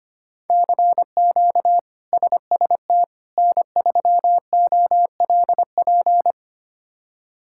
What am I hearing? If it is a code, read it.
Morse code, heard as a keyed tone, about 25 wpm, 700 Hz. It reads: CQ SST N3OLP
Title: K1USN SST QSO Practice - 25wpm